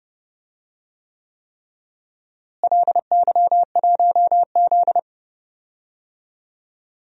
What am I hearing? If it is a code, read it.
Morse code, heard as a keyed tone, about 31 wpm, 700 Hz. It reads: LY1Z
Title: DXCC-100 Call Signs - 4 Characters - Repeated 3x - 30wpm